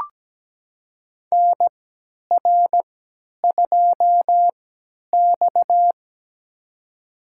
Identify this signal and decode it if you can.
Morse code, heard as a keyed tone, about 17 wpm, spaced as for 10 wpm, 700 Hz. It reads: NR2X
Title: USA Call Signs - 4 Characters - 3x Character Spacing - 17wpm